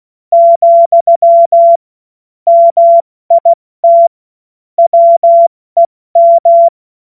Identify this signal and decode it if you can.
Morse code, heard as a keyed tone, about 16 wpm, 680 Hz. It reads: , MIT WEM
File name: Morsekurs / Witze / Witz 19. Oktober 2021, 70 F80